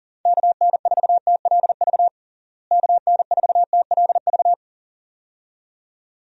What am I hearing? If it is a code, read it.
Morse code, heard as a keyed tone, about 41 wpm, 700 Hz. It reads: KN4TLV KN4TLV
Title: USA Call Signs - 6 Characters - Repeated 2x - 40wpm